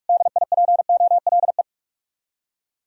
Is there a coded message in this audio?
Yes: Morse code from 0.1 to 1.6 s, heard as a keyed tone, about 58 wpm, 700 Hz. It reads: DIPOLE